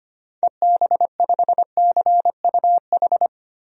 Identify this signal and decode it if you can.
Morse code, heard as a keyed tone, about 25 wpm, 700 Hz. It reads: EB5CUH
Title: DXCC-100 Call Signs - 6 Characters - Repeated 3x - 25wpm